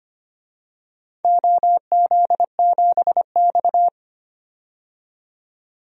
Morse code, heard as a keyed tone, about 25 wpm, 700 Hz. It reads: OZ7X
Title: DXCC-100 Call Signs - 4 Characters - 25wpm